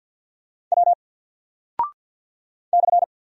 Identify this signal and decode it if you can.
Morse code, heard as a keyed tone, about 48 wpm, 700 Hz. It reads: W /
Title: Single Character (Letter:Number:Punctuation:Prosign) Alt 50wpm